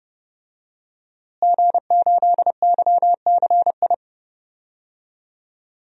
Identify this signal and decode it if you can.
Morse code, heard as a keyed tone, about 30 wpm, 700 Hz. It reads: G8YCI
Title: DXCC-100 Call Signs - 5 Characters - Repeated 2x - 30wpm